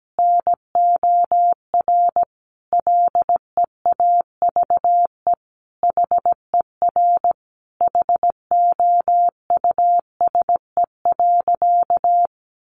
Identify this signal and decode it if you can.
Morse code, heard as a keyed tone, about 17 wpm, 700 Hz. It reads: NOR LEAVE HER HOUSE.
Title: Sentences from Top 1500 Words 17wpm